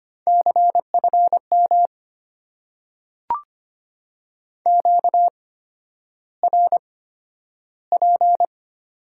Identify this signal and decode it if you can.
Morse code, heard as a keyed tone, about 25 wpm, spaced as for 11 wpm, 690 Hz. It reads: CFM QRP
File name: Top 90 QSO Elements - 8x Character Spacing - 25wpm